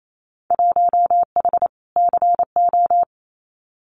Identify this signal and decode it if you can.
Morse code, heard as a keyed tone, about 28 wpm, 700 Hz. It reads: 1H CO
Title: Winter Field Day Exchanges 28wpm